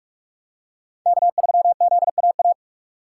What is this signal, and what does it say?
Morse code, heard as a keyed tone, about 40 wpm, 700 Hz. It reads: K3ZAA